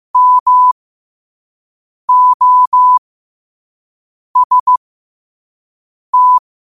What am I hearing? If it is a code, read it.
Morse code, heard as a keyed tone, about 15 wpm, spaced as for 5 wpm, 1000 Hz. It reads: MOST
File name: qotdfiles / 5wpm